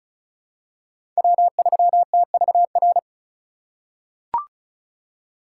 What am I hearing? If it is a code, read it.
Morse code, heard as a keyed tone, about 34 wpm, 700 Hz. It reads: W3TVR
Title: USA Call Signs - 5 Characters - 3x Character Spacing - 35wpm